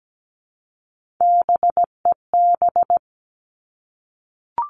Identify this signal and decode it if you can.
Morse code, heard as a keyed tone, about 17 wpm, 700 Hz. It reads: BEB
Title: All Cut Numbers - 3-Digit 17wpm